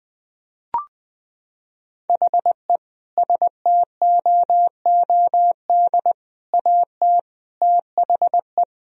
Morse code, heard as a keyed tone, about 20 wpm, 700 Hz. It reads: HE STOOD AT THE
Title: Sentences from Top 1000 Words - Review Part 3 - 20wpm